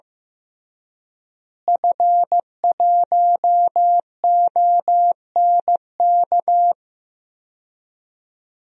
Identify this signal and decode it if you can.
Morse code, heard as a keyed tone, about 15 wpm, 700 Hz. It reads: F1ONK
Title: DXCC-100 Call Signs - 5 Characters - Repeated 2x - 15wpm